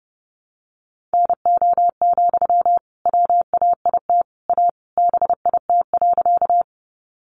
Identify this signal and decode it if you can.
Morse code, heard as a keyed tone, about 30 wpm, 700 Hz. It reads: NO, WAIT A BIT.
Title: Sentences from Top 900 Words 30wpm